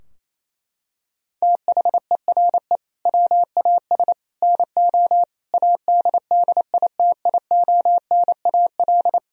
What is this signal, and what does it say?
Morse code, heard as a keyed tone, about 27 wpm, 700 Hz. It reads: THERE WAS NO ADDITIONAL
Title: Sentences from Top 1000 Words - Review Part 4 - 28wpm